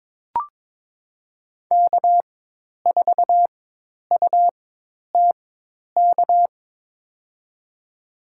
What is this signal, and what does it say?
Morse code, heard as a keyed tone, about 22 wpm, spaced as for 10 wpm, 700 Hz. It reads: K4UTK
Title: USA Call Signs - 5 Characters - 4x Character Spacing - 22wpm